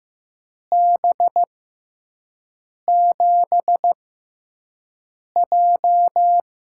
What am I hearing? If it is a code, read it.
Morse code, heard as a keyed tone, about 15 wpm, spaced as for 5 wpm, 700 Hz. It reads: B7J
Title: USA Call Signs - 4 Characters - 6x Character Spacing - 15wpm